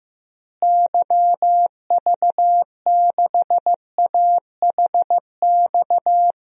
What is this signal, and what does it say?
Morse code, heard as a keyed tone, about 16 wpm, 700 Hz. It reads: YV6AHX
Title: DXCC-100 Call Signs - 6 Characters - Repeated 3x - 15wpm